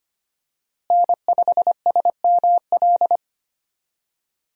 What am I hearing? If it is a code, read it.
Morse code, heard as a keyed tone, about 25 wpm, 700 Hz. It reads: N5SML
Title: USA Call Signs - 5 Characters - Repeated 2x - 25wpm